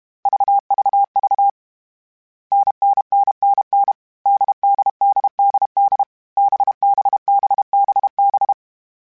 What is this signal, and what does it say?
Morse code, heard as a keyed tone, about 32 wpm, 800 Hz. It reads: VVV NNNNN DDDDD BBBBB
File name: Variable Speed / Variable Speed-mp3s / GZ GZ7 sp @30